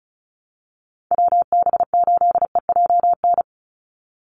Morse code, heard as a keyed tone, about 34 wpm, 700 Hz. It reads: WB8EJN
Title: Call Signs Repeated 3x - 35wpm